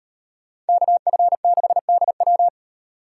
Morse code, heard as a keyed tone, about 35 wpm, 700 Hz. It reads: KF6DW